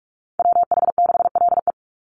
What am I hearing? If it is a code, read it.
Morse code, heard as a keyed tone, about 44 wpm, 700 Hz. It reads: WH6LE
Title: NA SOTA Chaser Callsigns 45wpm